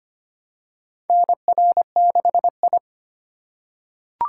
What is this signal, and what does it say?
Morse code, heard as a keyed tone, about 25 wpm, 700 Hz. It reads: NR6I
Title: USA Call Signs - 4 Characters - 3x Character Spacing - 25wpm